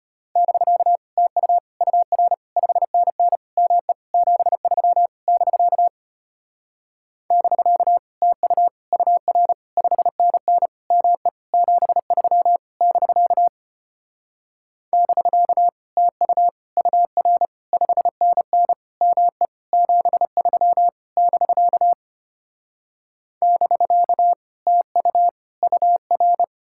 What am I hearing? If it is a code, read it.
Morse code, heard as a keyed tone, about 33 wpm, 700 Hz. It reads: <BK> TU UR 5NN ME 73 <BK> <BK> TU UR 5NN ME 73 <BK> <BK> TU UR 5NN ME 73 <BK> <BK> TU UR